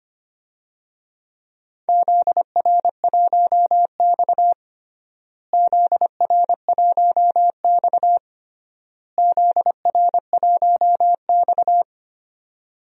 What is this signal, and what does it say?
Morse code, heard as a keyed tone, about 25 wpm, 700 Hz. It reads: ZR1X ZR1X ZR1X
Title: DXCC-100 Call Signs - 4 Characters - Repeated 3x - 25wpm